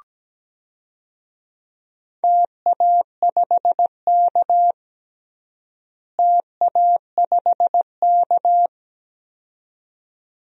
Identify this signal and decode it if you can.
Morse code, heard as a keyed tone, about 17 wpm, 700 Hz. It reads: TA5K TA5K
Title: DXCC-100 Call Signs - 4 Characters - Repeated 2x - 17wpm